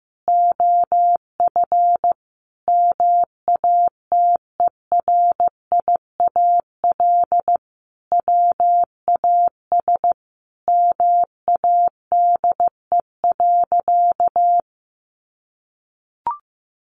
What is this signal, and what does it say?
Morse code, heard as a keyed tone, about 15 wpm, 700 Hz. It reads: OF MATERIAL WAS MADE.
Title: Sentences from Top 1200 Words 15wpm